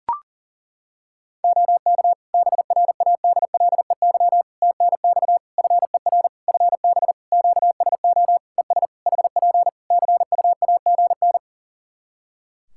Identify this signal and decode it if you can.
Morse code, heard as a keyed tone, about 40 wpm, 700 Hz. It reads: OK BRADLEY TNX FER FB QSO ES HP CUAGN
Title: Practice QSO - Exchange 4 - Ending - 40wpm